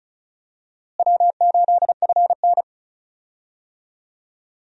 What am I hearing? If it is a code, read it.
Morse code, heard as a keyed tone, about 36 wpm, 700 Hz. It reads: W8FN
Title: USA Call Signs - 4 Characters - Repeated 3x - 35wpm